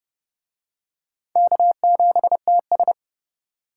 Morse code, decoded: K7TS

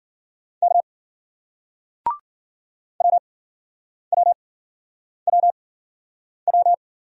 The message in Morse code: K WWWW

